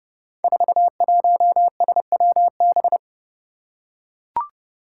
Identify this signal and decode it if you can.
Morse code, heard as a keyed tone, about 30 wpm, 700 Hz. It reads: 41SWB
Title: Five Challenging Letter-Number Combinations 30wpm